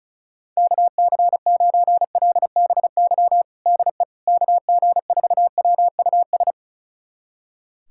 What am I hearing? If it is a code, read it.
Morse code, heard as a keyed tone, about 36 wpm, 700 Hz. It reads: KC9LBY DE KG4WUS
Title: USA Callsign DE Callsign - 6 Characters - 35wpm